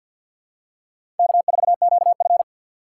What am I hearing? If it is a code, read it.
Morse code, heard as a keyed tone, about 52 wpm, 700 Hz. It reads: K4QF
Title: USA Call Signs - 4 Characters - 3x Character Spacing - 50wpm